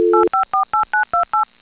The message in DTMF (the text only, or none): 8548928